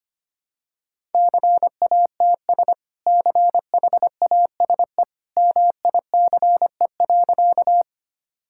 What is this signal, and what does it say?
Morse code, heard as a keyed tone, about 25 wpm, 700 Hz. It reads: CATS CHASE MICE.
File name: Easy Sentences with Hints - 25wpm